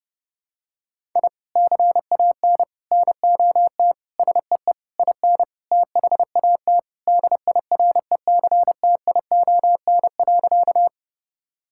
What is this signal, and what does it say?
Morse code, heard as a keyed tone, about 32 wpm, 700 Hz. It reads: I CAN NOT SEE IN THAT DIRECTION.
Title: Sentences from Top 1000 Words - Review Part 5 - 30wpm